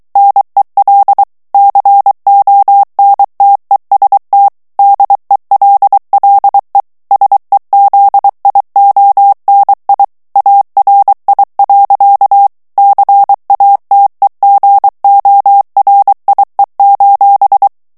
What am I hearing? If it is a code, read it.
Morse code, heard as a keyed tone, about 23 wpm, 800 Hz. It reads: NEL CONTEST DELLE SEZIONI ARI. CATEGORIE:
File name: ITALIEN texte clair 2 vitesse 22 mots minute